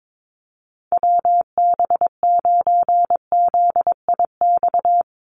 Morse code, heard as a keyed tone, about 22 wpm, 700 Hz. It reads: WB9ZIX
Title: USA Call Signs 22wpm